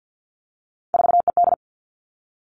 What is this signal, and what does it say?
Morse code, heard as a keyed tone, about 50 wpm, 700 Hz. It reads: 4ED
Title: Three Challenging Letter-Number Combinations 50wpm